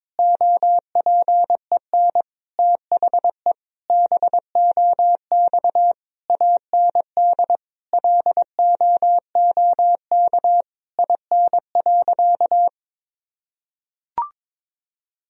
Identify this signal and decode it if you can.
Morse code, heard as a keyed tone, about 22 wpm, 700 Hz. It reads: OPEN THE BOX AND LOOK IN.
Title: Sentences from Top 1000 Words - Review Part 2 - 22wpm